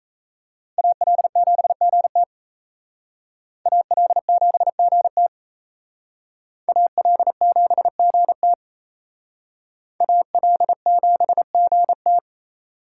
Morse code, heard as a keyed tone, about 34 wpm, 700 Hz. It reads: AL7GT AL7GT AL7GT AL7GT
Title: Speed Racing - USA 2x2 Call Signs 28wpm